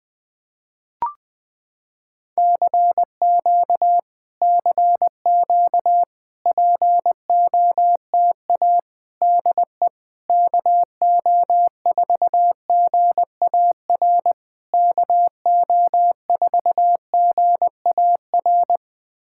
Morse code, heard as a keyed tone, about 20 wpm, 700 Hz. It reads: CQ CQ POTA DE KO4GAR KO4GAR
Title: POTA QSO Practice - 20wpm